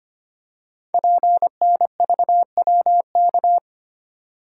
Morse code, heard as a keyed tone, about 25 wpm, 700 Hz. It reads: PNVWK
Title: Code Groups - 5 Letters - 25wpm